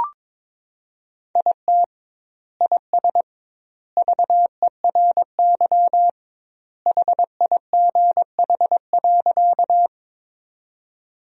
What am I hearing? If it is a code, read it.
Morse code, heard as a keyed tone, about 22 wpm, 700 Hz. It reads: IT IS VERY HIGH.
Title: 2x Word-Spacing - Sentences from Top 200 Words - 22wpm